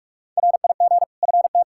Morse code, heard as a keyed tone, about 41 wpm, 700 Hz. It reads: RIG FT
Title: K1USN SST QSO Practice 45wpm